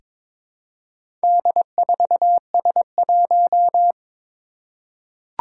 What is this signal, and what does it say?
Morse code, heard as a keyed tone, about 22 wpm, 700 Hz. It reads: D4S1
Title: Four Challenging Letter-Number Combinations 22wpm